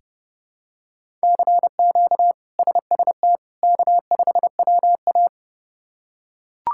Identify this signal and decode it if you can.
Morse code, heard as a keyed tone, about 30 wpm, 700 Hz. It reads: CQ SST K5WA
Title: K1USN SST QSO Practice - 30wpm